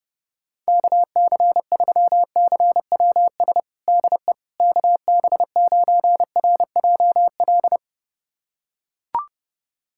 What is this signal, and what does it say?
Morse code, heard as a keyed tone, about 30 wpm, 700 Hz. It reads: KC3CWS DE KB9RJL
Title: USA Callsign DE Callsign - 6 Characters - 30wpm